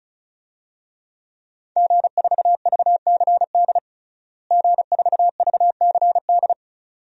Morse code, heard as a keyed tone, about 36 wpm, 700 Hz. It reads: G4VCD G4VCD
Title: DXCC-100 Call Signs - 5 Characters - Repeated 2x - 35wpm